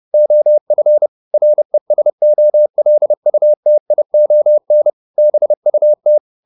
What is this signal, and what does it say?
Morse code, heard as a keyed tone, about 30 wpm, 600 Hz. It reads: OF RESOLUTION BUT